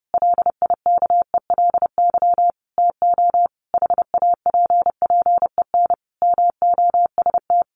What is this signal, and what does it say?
Morse code, heard as a keyed tone, about 30 wpm, 700 Hz. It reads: LIKELY TO HAPPEN MOST
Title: Sentences from Top 1300 Words 30wpm